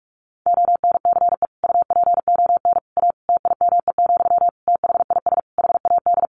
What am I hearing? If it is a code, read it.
Morse code, heard as a keyed tone, about 44 wpm, 700 Hz. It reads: ONCE UPON A TIME, THIS HAD
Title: Sentences from Top 1500 Words 45wpm